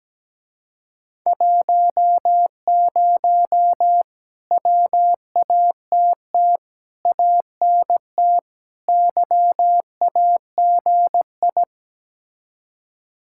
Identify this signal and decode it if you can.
Morse code, heard as a keyed tone, about 17 wpm, 700 Hz. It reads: 10 WATT ANT YAGI
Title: Speed-Racing Common QSO Phrases - 17wpm